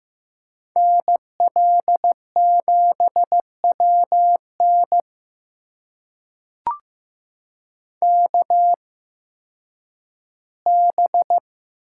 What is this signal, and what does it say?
Morse code, heard as a keyed tone, about 15 wpm, spaced as for 10 wpm, 700 Hz. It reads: NL7WN KB
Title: USA Call Signs - 8x Character Spacing - 15wpm